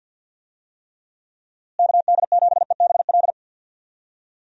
Morse code, heard as a keyed tone, about 52 wpm, 700 Hz. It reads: KD7EBL
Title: USA Call Signs - 6 Characters - Repeated 3x - 50wpm